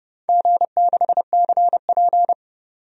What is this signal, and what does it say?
Morse code, heard as a keyed tone, about 31 wpm, 700 Hz. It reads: G6CP
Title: DXCC-100 Call Signs - 4 Characters - 30wpm